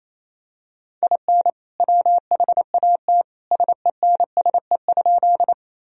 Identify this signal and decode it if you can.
Morse code, heard as a keyed tone, about 28 wpm, 700 Hz. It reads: IN WHAT SENSE?
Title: Sentences from Top 1000 Words - Review Part 4 - 28wpm